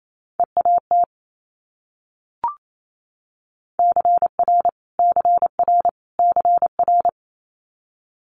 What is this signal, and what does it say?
Morse code, heard as a keyed tone, about 28 wpm, 700 Hz. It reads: EAT CR CR CR
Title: Multi-letter Phonemes Repeated 3x 28wpm